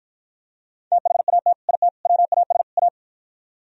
Morse code, heard as a keyed tone, about 49 wpm, 700 Hz. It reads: THAT IT WAS A